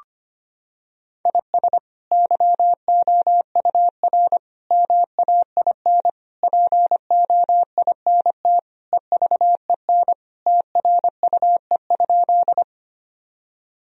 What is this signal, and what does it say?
Morse code, heard as a keyed tone, about 25 wpm, 700 Hz. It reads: IS YOUR MAIN POINT EVEN TRUE?
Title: Sentences from Top 1000 Words - Review Part 2 - 25wpm